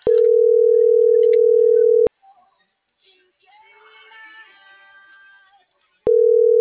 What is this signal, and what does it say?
Ringback tone (North America), 440 + 480 Hz.